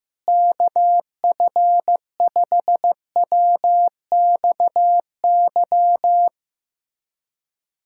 Morse code, heard as a keyed tone, about 15 wpm, 700 Hz. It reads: KF5WXY
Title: USA Callsign DE Callsign 15wpm